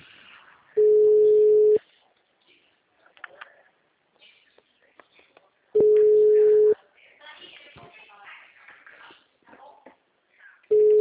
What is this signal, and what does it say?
Ringback tone (Europe, 425 Hz), with speech.